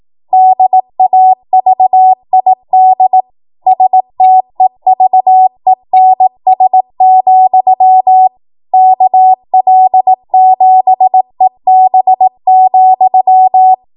Morse code, heard as a keyed tone, about 18 wpm, 750 Hz. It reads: DAVID STEVENS, KL7EB,
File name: Morse / 18WPM